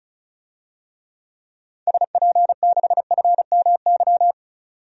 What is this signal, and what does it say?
Morse code, heard as a keyed tone, about 36 wpm, 700 Hz. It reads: SP6FMY
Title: DXCC-100 Call Signs - 6 Characters - 35wpm